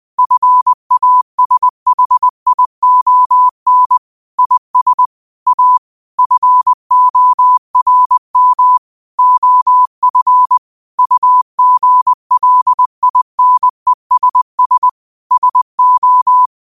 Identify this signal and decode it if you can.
Morse code, heard as a keyed tone, about 20 wpm, 1000 Hz. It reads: FASHION IS A FORM OF UGLINESS SO